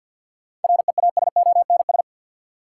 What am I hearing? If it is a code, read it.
Morse code, heard as a keyed tone, about 51 wpm, 700 Hz. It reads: REASONS
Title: Sets of 4 Words - New Word Drill - 50wpm